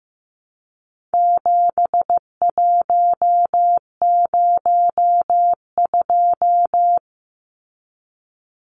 Morse code, heard as a keyed tone, about 15 wpm, 700 Hz. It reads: 7102
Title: Numbers - 4-Digit 15wpm